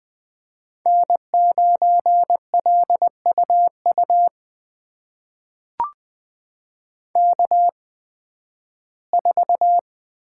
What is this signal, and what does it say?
Morse code, heard as a keyed tone, about 20 wpm, spaced as for 13 wpm, 700 Hz. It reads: N9LUU K4
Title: USA Call Signs - 5 Characters - 8x Character Spacing - 20wpm